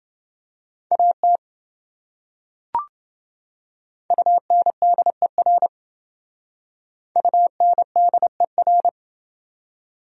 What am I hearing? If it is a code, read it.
Morse code, heard as a keyed tone, about 27 wpm, 700 Hz. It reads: AT UNDER UNDER